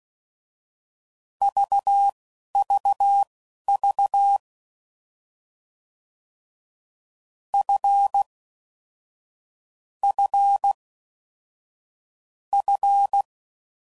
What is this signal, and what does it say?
Morse code, heard as a keyed tone, about 16 wpm, spaced as for 11 wpm, 800 Hz. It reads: VVV F F F